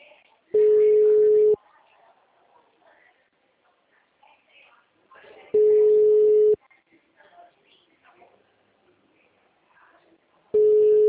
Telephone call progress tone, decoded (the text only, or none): ringback tone